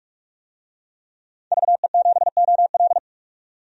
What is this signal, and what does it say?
Morse code, heard as a keyed tone, about 44 wpm, 690 Hz. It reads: VE7OL
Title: DXCC-100 Call Signs - 5 Characters - 45wpm